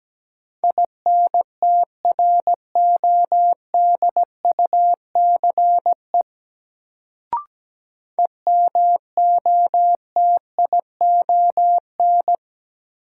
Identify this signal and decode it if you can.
Morse code, heard as a keyed tone, about 17 wpm, 700 Hz. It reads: INTRODUCE EMOTION
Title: Sentences from Top 1600 Words - New Word Drill - 17wpm